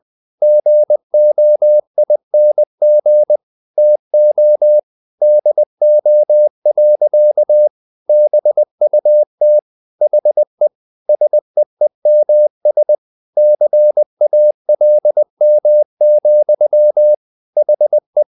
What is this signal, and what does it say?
Morse code, heard as a keyed tone, about 20 wpm, 600 Hz. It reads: GOING TO DO. BUT HE SEEMS CALM, HE
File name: the old man and the sea / the old man and the sea part 1 20wpm